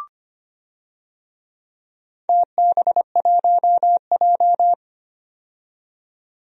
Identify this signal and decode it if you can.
Morse code, heard as a keyed tone, about 25 wpm, 700 Hz. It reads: TB1J
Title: DXCC-100 Call Signs - 4 Characters - 25wpm